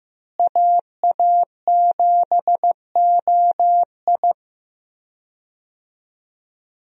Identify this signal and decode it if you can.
Morse code, heard as a keyed tone, about 15 wpm, 700 Hz. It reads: AA7OI
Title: USA Call Signs - 5 Characters - Repeated 3x - 15wpm